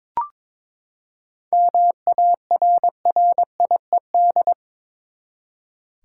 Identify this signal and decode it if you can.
Morse code, heard as a keyed tone, about 23 wpm, 700 Hz. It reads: MARRIED
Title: Seven Letter Words - 22wpm